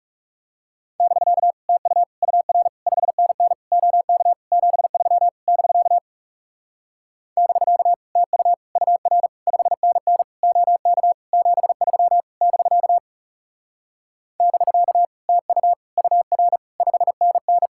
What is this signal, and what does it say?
Morse code, heard as a keyed tone, about 38 wpm, 700 Hz. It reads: <BK> TU UR 5NN OK 73 <BK> <BK> TU UR 5NN OK 73 <BK> <BK> TU UR 5NN